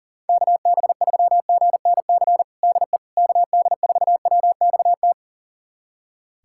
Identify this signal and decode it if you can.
Morse code, heard as a keyed tone, about 37 wpm, 700 Hz. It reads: KB3GNC DE KD4WXT
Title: USA Callsign DE Callsign - 6 Characters - 40wpm